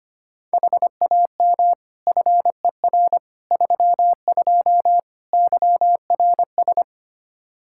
Morse code, heard as a keyed tone, about 25 wpm, 700 Hz. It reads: HAM FER 32 YRS
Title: Practice QSO - Exchange 3 - More Info - 25wpm